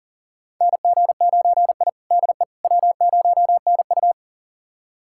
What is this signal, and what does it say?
Morse code, heard as a keyed tone, about 40 wpm, 700 Hz. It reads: NG9I DE W0NU
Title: USA Callsign DE Callsign - 4 Characters - 40wpm